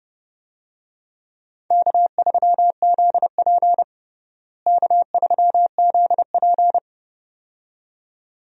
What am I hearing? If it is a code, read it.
Morse code, heard as a keyed tone, about 32 wpm, 700 Hz. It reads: K3ZP K3ZP
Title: USA Call Signs - 4 Characters - Repeated 2x - 30wpm